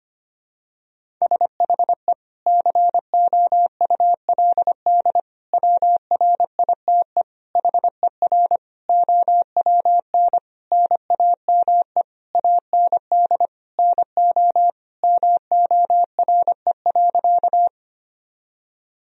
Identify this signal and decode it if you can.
Morse code, heard as a keyed tone, about 25 wpm, 700 Hz. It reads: SHE COULD WRITE HER OWN NAME AND NO MORE.
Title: Sentences from Top 1000 Words - Review Part 1 - 25wpm